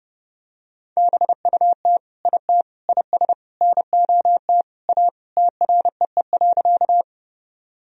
Morse code, heard as a keyed tone, about 30 wpm, 700 Hz. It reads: BUT IT IS NOT A TREE.
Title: Sentences from Top 1000 Words - Review Part 1 - 30wpm